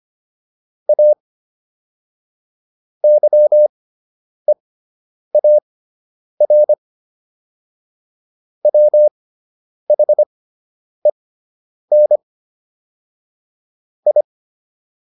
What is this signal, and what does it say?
Morse code, heard as a keyed tone, about 25 wpm, spaced as for 9 wpm, 600 Hz. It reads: A YEAR WHEN I